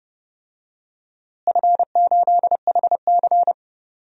Morse code, heard as a keyed tone, about 30 wpm, 700 Hz. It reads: F8HC
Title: DXCC-100 Call Signs - 4 Characters - Repeated 3x - 30wpm